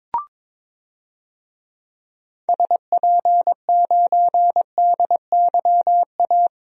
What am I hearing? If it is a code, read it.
Morse code, heard as a keyed tone, about 22 wpm, 700 Hz. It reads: SP9DYA
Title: DXCC-100 Call Signs - 6 Characters - 22wpm